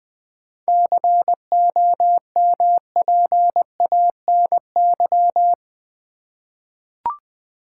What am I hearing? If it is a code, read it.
Morse code, heard as a keyed tone, about 19 wpm, 700 Hz. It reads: COMPANY
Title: Seven Letter Words - 20wpm